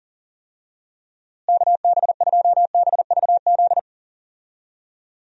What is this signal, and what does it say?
Morse code, heard as a keyed tone, about 40 wpm, 700 Hz. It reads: KB2BVZ
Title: USA Call Signs - 6 Characters - Repeated 2x - 40wpm